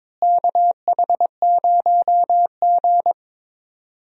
Morse code, heard as a keyed tone, about 22 wpm, 700 Hz. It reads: KH0G
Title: USA Call Signs - 4 Characters - Repeated 3x - 22wpm